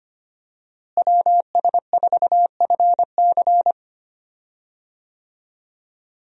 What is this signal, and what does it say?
Morse code, heard as a keyed tone, about 25 wpm, 700 Hz. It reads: WS4FC
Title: USA Call Signs - 5 Characters - Repeated 3x - 25wpm